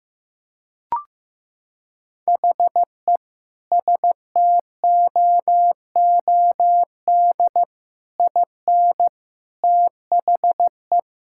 Morse code, heard as a keyed tone, about 14 wpm, 700 Hz. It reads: HE STOOD IN THE
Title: Sentences from Top 1000 Words - Review Part 6 - 15wpm